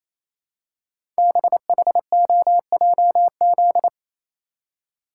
Morse code, heard as a keyed tone, about 29 wpm, 700 Hz. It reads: BHOJZ
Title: Code Groups - 5 Letters - 28wpm